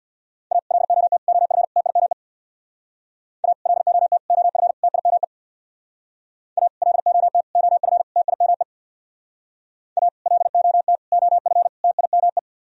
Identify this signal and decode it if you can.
Morse code, heard as a keyed tone, about 60 wpm, 700 Hz. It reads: A LOT OF TIME A LOT OF TIME A LOT OF TIME A LOT OF TIME
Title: Speed-Racing - Sets of 4 Words 50wpm